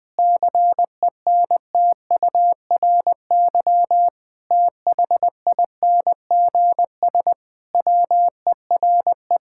Morse code, heard as a keyed tone, about 20 wpm, 700 Hz. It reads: CENTURY THINGS WERE
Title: Sentences with 6-Word Phrases - 20wpm